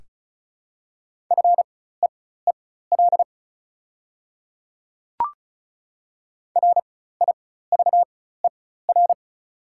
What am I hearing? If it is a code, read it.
Morse code, heard as a keyed tone, about 36 wpm, spaced as for 16 wpm, 700 Hz. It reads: FEEL RIVER
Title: 4x Character Spacing - Top 500 Words - 35wpm